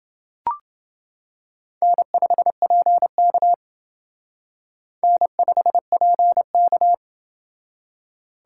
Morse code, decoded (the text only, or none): N5PK N5PK